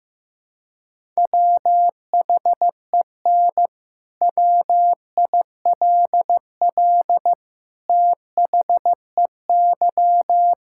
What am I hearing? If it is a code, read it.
Morse code, heard as a keyed tone, about 15 wpm, 700 Hz. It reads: WHEN WILL THEY
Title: Sentences from Top 1000 Words - Review Part 5 - 15wpm